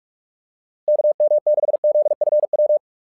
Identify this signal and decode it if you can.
Morse code, heard as a keyed tone, about 44 wpm, 600 Hz. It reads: KM6ZFW
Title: SOTA QSO Practice 45wpm